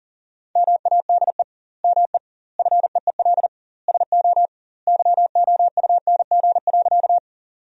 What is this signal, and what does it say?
Morse code, heard as a keyed tone, about 37 wpm, 700 Hz. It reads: MADE ME FEEL SO YOUNG.
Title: 2x Word-Spacing - Sentences from Top 400 Words - 40wpm